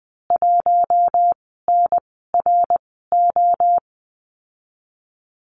Morse code, heard as a keyed tone, about 19 wpm, spaced as for 14 wpm, 700 Hz. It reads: 1NRO
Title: 055 Lesson 10g - Letter r - Call Signs 20wpm